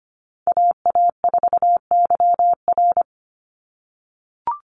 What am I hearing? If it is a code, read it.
Morse code, heard as a keyed tone, about 25 wpm, 700 Hz. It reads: AA4YR
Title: USA Call Signs 25wpm